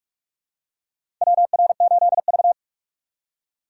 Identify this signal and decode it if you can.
Morse code, heard as a keyed tone, about 42 wpm, 700 Hz. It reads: WR8V
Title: USA Call Signs - 4 Characters - 4x Character Spacing - 45wpm